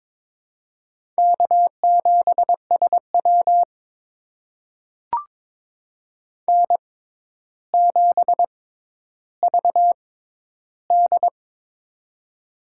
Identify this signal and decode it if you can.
Morse code, heard as a keyed tone, about 22 wpm, spaced as for 11 wpm, 700 Hz. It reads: K7SW N7VD
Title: USA Call Signs - 4 Characters - 6x Character Spacing - 22wpm